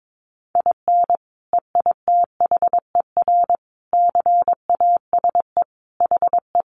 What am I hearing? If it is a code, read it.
Morse code, heard as a keyed tone, about 22 wpm, 700 Hz. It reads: IN EITHER CASE HE
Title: Sentences from Top 1300 Words 22wpm